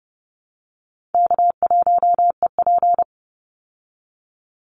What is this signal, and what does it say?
Morse code, heard as a keyed tone, about 30 wpm, 700 Hz. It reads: K1EP
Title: Call Signs Repeated 3x - 30wpm